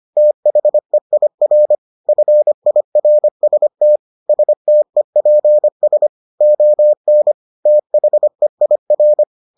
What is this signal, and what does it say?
Morse code, heard as a keyed tone, about 25 wpm, 600 Hz. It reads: THEIR FIRST STEPS ON THEIR